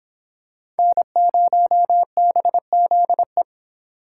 Morse code, heard as a keyed tone, about 26 wpm, 700 Hz. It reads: N0BZE